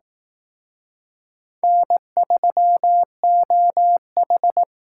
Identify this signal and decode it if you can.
Morse code, heard as a keyed tone, about 18 wpm, 700 Hz. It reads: N3OH
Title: Speed Racing - USA 1x2 Call Signs 15wpm